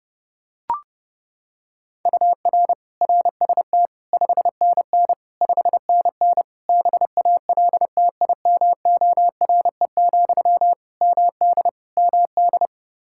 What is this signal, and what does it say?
Morse code, heard as a keyed tone, about 30 wpm, 700 Hz. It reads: UR RST 5NN 5NN BALTIMORE, MD MD
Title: Practice QSO - Exchange 1 - Beginning - 30wpm